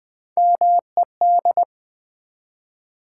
Morse code, heard as a keyed tone, about 20 wpm, 700 Hz. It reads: MED
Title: Build A Word - 1 Letter At A Time - 3 to 5-Letter Words - 20wpm